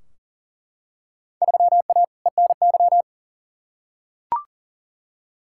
Morse code, heard as a keyed tone, about 40 wpm, 700 Hz. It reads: 3A ENY
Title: Field Day Exchanges - 40wpm